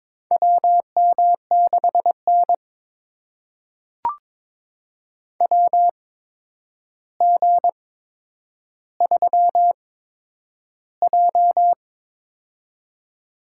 Morse code, heard as a keyed tone, about 22 wpm, spaced as for 9 wpm, 700 Hz. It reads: WM6N WG3J
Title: USA Call Signs - 8x Character Spacing - 22wpm